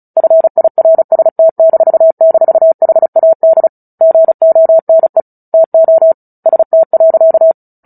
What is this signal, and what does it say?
Morse code, heard as a keyed tone, about 34 wpm, 650 Hz. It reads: FIRST--HAD GONE TO ST.
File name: Book Two / 09-Chapter 9 - 35 WPM